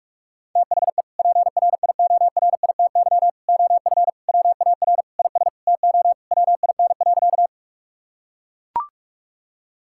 Morse code, heard as a keyed tone, about 42 wpm, 700 Hz. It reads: THE PRIORITY OF WAR IS TO WIN.
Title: Sentences from Top 1600 Words 45wpm